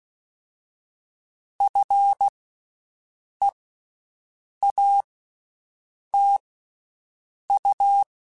Morse code, heard as a keyed tone, about 16 wpm, spaced as for 6 wpm, 800 Hz. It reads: FEATU
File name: Special INGs-mp3s / Special INGs-mp3s / Slow ing @15